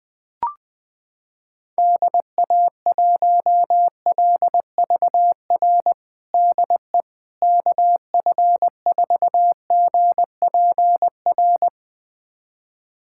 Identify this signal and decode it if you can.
Morse code, heard as a keyed tone, about 20 wpm, 700 Hz. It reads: DA1LVR DE KF4GPR
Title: USA Callsign DE Callsign - 6 Characters - 20wpm